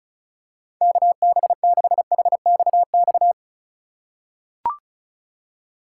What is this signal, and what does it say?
Morse code, heard as a keyed tone, about 34 wpm, 700 Hz. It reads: KB6HXX
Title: USA Call Signs - 6 Characters - 4x Character Spacing - 35wpm